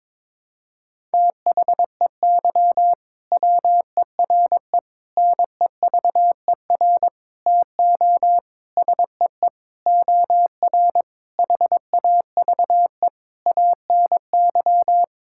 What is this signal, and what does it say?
Morse code, heard as a keyed tone, about 22 wpm, 700 Hz. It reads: THEY WERE NEVER TO SEE OR HAVE ANY
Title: Sentences from Top 1000 Words - Review Part 6 - 22wpm